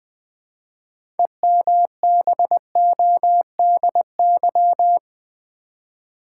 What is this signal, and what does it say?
Morse code, heard as a keyed tone, about 20 wpm, 700 Hz. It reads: EMBODY
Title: 6 Letter Words - First 3 Letters Given - 20wpm